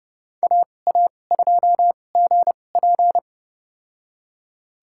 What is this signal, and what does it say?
Morse code, heard as a keyed tone, about 30 wpm, spaced as for 22 wpm, 690 Hz. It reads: AA2GP
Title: USA Call Signs - 5 Characters - 2x Character Spacing - 30wpm